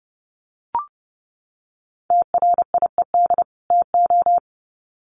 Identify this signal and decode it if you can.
Morse code, heard as a keyed tone, about 30 wpm, 700 Hz. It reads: TRIED TO
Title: Sets of 2 Words 30wpm